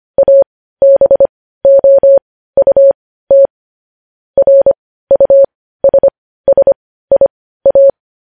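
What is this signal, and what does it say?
Morse code, heard as a keyed tone, about 25 wpm, spaced as for 15 wpm, 550 Hz. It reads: ABOUT RUSSIA
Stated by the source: Morse news headlines practice sent at 15 WPM.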